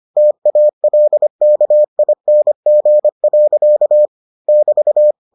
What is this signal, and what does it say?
Morse code, heard as a keyed tone, about 25 wpm, 600 Hz. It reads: TALKING. <BT>